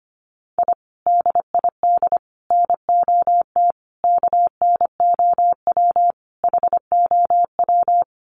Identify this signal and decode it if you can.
Morse code, heard as a keyed tone, about 25 wpm, 700 Hz. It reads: I DID NOT KNOW HOW
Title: Sets of 5 Words Encore 25wpm